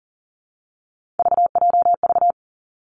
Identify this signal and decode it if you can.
Morse code, heard as a keyed tone, about 41 wpm, 700 Hz. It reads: VJV
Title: Three Challenging Letter-Number Combinations 40wpm